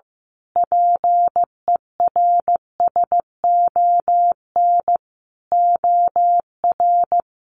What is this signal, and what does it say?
Morse code, heard as a keyed tone, about 15 wpm, 700 Hz. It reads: PERSON OR
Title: Sentences from Top 1400 Words 15wpm